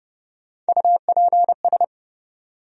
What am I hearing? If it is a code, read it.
Morse code, heard as a keyed tone, about 30 wpm, 700 Hz. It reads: UPS
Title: USA Call Sign Postfixes - 3 Letters 30wpm